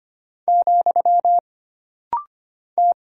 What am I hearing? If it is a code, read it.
Morse code, heard as a keyed tone, about 25 wpm, 700 Hz. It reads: , T
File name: Single Character (Letter:Number:Punctuation:Prosign) - Rapid-Fire - 25wpm